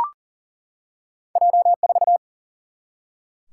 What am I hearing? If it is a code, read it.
Morse code, heard as a keyed tone, about 40 wpm, 700 Hz. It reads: J 4
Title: Single-Letter Single-Number 40wpm